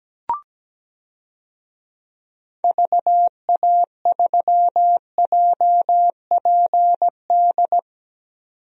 Morse code, heard as a keyed tone, about 17 wpm, 700 Hz. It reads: VA3JPD
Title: DXCC-100 Call Signs - 6 Characters - 17wpm